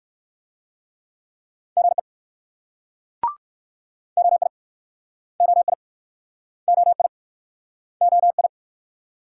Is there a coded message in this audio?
Yes: Morse code from 1.8 to 8.5 s, heard as a keyed tone, about 64 wpm, 690 Hz. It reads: GE OI OI OI OI